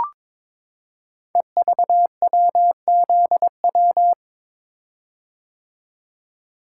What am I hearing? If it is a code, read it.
Morse code, heard as a keyed tone, about 21 wpm, 700 Hz. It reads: EVWZW
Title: Code Groups - 5 Letter:Numbers - 22wpm